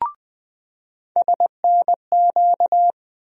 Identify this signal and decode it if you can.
Morse code, heard as a keyed tone, about 20 wpm, 700 Hz. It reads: SNQ
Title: USA Call Sign Postfixes - 3 Letters 20wpm